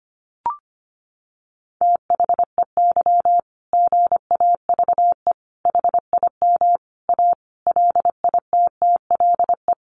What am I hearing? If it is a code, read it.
Morse code, heard as a keyed tone, about 25 wpm, 700 Hz. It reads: THEY GAVE HIM A LITTLE
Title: Sentences from Top 1500 Words 25wpm